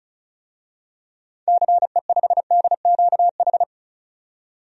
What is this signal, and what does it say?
Morse code, heard as a keyed tone, about 34 wpm, 700 Hz. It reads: CE5DQH
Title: DXCC-100 Call Signs - 6 Characters - 35wpm